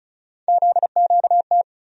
Morse code, heard as a keyed tone, about 36 wpm, 700 Hz. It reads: ZQT
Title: USA Call Sign Postfixes - 3 Letters 35wpm